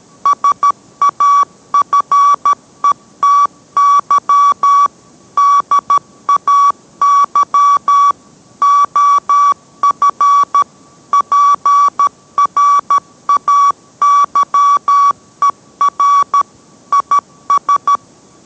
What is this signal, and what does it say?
Morse code, heard as a keyed tone, about 13 wpm, 1140 Hz. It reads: SAFETY DAY OF PRAYER IS